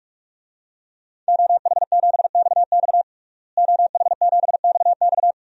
Morse code, heard as a keyed tone, about 46 wpm, 700 Hz. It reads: OH7XX OH7XX
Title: DXCC-100 Call Signs - 5 Characters - Repeated 2x - 45wpm